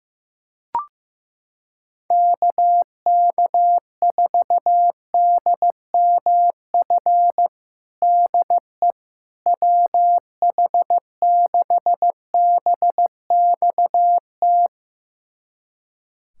Morse code, heard as a keyed tone, about 15 wpm, 700 Hz. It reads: KK4DMF DE WH6BXT
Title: USA Callsign DE Callsign 15wpm